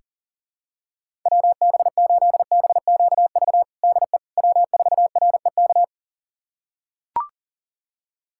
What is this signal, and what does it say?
Morse code, heard as a keyed tone, about 40 wpm, 700 Hz. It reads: WB8BQV DE W4REK
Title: USA Callsign DE Callsign 40wpm